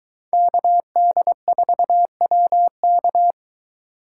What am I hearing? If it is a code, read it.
Morse code, heard as a keyed tone, about 23 wpm, 700 Hz. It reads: KD4WK